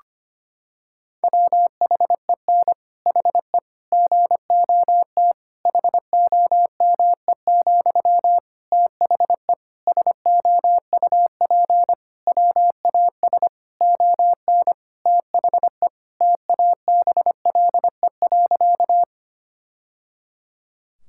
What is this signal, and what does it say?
Morse code, heard as a keyed tone, about 25 wpm, 700 Hz. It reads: WHEN HE GOT HOME, THE SOUP WAS ON THE TABLE.
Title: Sentences from Top 1000 Words - Review Part 6 - 25wpm